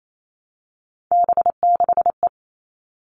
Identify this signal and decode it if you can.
Morse code, heard as a keyed tone, about 28 wpm, 700 Hz. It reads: B6E
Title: All Cut Numbers - 3-Digit 28wpm